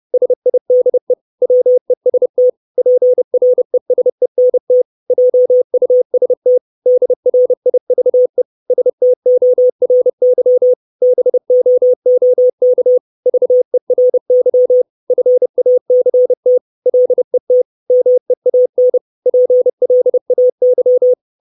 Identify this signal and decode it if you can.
Morse code, heard as a keyed tone, about 30 wpm, 500 Hz. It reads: SIDE WEST PRESENT JUST DRIVE STORY BOOK VERY FACT LET MEAN PLAY
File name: 632 Words 30WPM